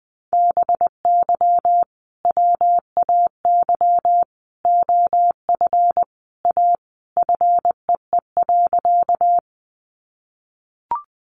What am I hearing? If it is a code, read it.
Morse code, heard as a keyed tone, about 20 wpm, 700 Hz. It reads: BY WAY OF A FEE.
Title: Sentences from Top 1500 Words 20wpm